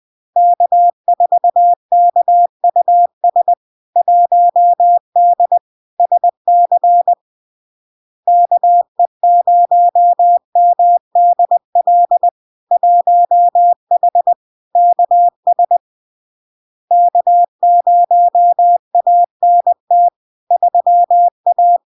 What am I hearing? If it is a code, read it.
Morse code, heard as a keyed tone, about 20 wpm, 700 Hz. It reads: K4KUS 1D SC KE0MDL 1H KS K0ANT 3A
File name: LARK-FD-2022-Log-in-CW-20wpm-700-Hz